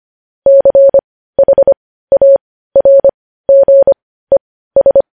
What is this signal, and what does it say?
Morse code, heard as a keyed tone, about 25 wpm, spaced as for 15 wpm, 550 Hz. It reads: CHARGES